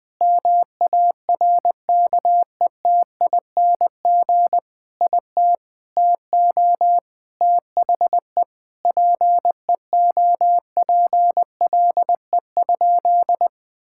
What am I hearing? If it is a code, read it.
Morse code, heard as a keyed tone, about 20 wpm, 700 Hz. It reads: MARKETING IT TO THE PEOPLE?
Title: Sentences from Top 1000 Words - Review Part 5 - 20wpm